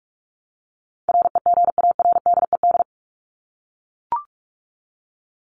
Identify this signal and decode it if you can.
Morse code, heard as a keyed tone, about 44 wpm, 700 Hz. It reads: REGARDED
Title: ED Suffix - Top 100 Words with Base Words 45wpm